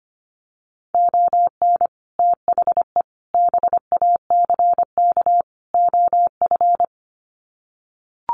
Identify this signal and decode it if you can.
Morse code, heard as a keyed tone, about 25 wpm, 700 Hz. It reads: ON THE BACK OF
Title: Sets of 4 Words 25wpm